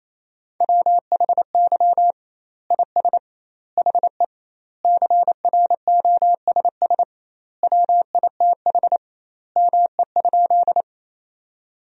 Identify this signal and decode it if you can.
Morse code, heard as a keyed tone, about 27 wpm, 700 Hz. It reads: WHY IS HE CROSS WITH ME?
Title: 2x Word-Spacing - Sentences from Top 300 Words - 28wpm